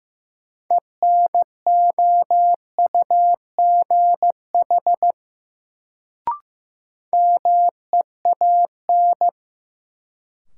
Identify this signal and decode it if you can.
Morse code, heard as a keyed tone, about 15 wpm, 700 Hz. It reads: ENOUGH MEAN
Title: Sets of 2 Words - New Word Drill - 15wpm